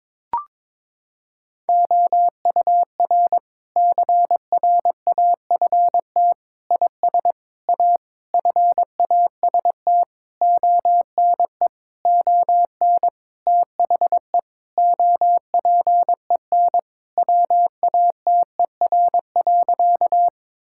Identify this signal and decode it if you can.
Morse code, heard as a keyed tone, about 22 wpm, 700 Hz. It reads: OUR CRAFT IS A FAST ONE ON THE OPEN WATER.
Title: Sentences from Top 1000 Words - Review Part 6 - 22wpm